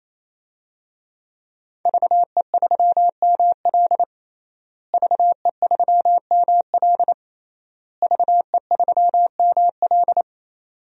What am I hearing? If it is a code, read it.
Morse code, heard as a keyed tone, about 28 wpm, 700 Hz. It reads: VE3ML VE3ML VE3ML
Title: DXCC-100 Call Signs - 5 Characters - Repeated 3x - 28wpm